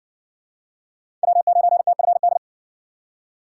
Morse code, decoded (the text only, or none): W9TFD